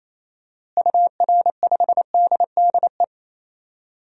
Morse code, heard as a keyed tone, about 28 wpm, 700 Hz. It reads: UR5DDE
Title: DXCC-100 Call Signs - 6 Characters - Repeated 3x - 28wpm